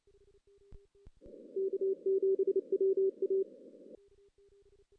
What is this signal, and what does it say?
Morse code, heard as a keyed tone, about 30 wpm, 410 Hz. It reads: K7WA
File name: 2016 JAN CW NAQP / 2016 Jan naqp k5ot - 0-6